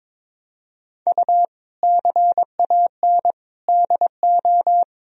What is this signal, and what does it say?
Morse code, heard as a keyed tone, about 21 wpm, 700 Hz. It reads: U CAN DO
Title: Sentences from Top 100 Words and Abbreviations 22wpm